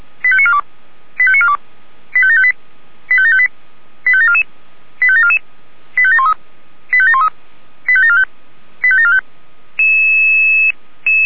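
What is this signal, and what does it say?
Five-tone selective call (ZVEI), heard as quick runs of five tones, tones of about 70 ms. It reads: 85942 85942 85758 85758 85744 85744 85713 85713 85745 85745
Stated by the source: ZVEI five-tone call 85865 received by SDR.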